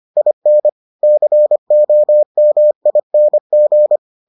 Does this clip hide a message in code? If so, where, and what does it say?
Yes: Morse code from 0.2 to 4.0 s, heard as a keyed tone, about 25 wpm, 600 Hz. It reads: IN COMING